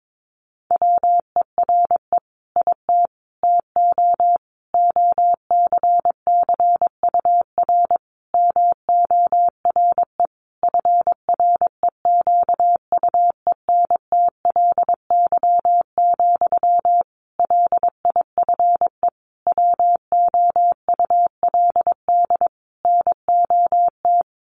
Morse code, heard as a keyed tone, about 22 wpm, 700 Hz. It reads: WERE IT TO OCCUR MORE FREQUENTLY, LIFE WOULD NOT
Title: Sentences from Top 1300 Words 22wpm